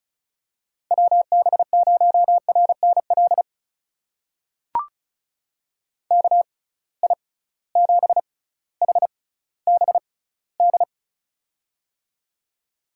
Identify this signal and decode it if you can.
Morse code, heard as a keyed tone, about 36 wpm, spaced as for 18 wpm, 700 Hz. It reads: WB0RNL KI7HBD
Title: USA Call Signs - 6 Characters - 6x Character Spacing - 35wpm